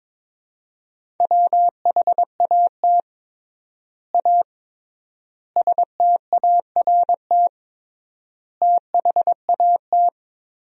Morse code, heard as a keyed tone, about 22 wpm, 700 Hz. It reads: WHAT A START THAT
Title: 3x Word-Spacing - Sentences from Top 300 Words - 22wpm